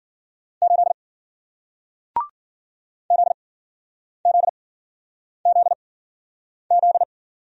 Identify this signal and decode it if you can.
Morse code, heard as a keyed tone, about 53 wpm, spaced as for 9 wpm, 700 Hz. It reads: 8 ZZZZ